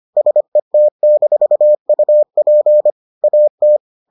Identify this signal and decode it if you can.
Morse code, heard as a keyed tone, about 25 wpm, 600 Hz. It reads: SET-UP AT